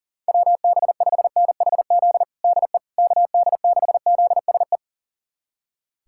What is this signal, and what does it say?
Morse code, heard as a keyed tone, about 40 wpm, 700 Hz. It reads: WB5NHZ DE KD6ZSE
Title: USA Callsign DE Callsign 40wpm